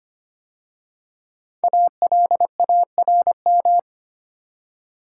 Morse code, heard as a keyed tone, about 25 wpm, 700 Hz. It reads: ALARM
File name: 5 Letter Words - First 3 Letters Given - 25wpm